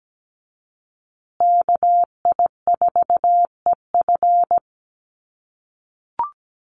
Morse code, heard as a keyed tone, about 17 wpm, 700 Hz. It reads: KI4EF
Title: USA Call Signs Repeated 2x 17wpm